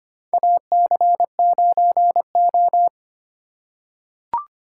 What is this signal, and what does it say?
Morse code, heard as a keyed tone, about 25 wpm, 690 Hz. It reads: AC9O
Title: USA Call Signs - 4 Characters - 4x Character Spacing - 25wpm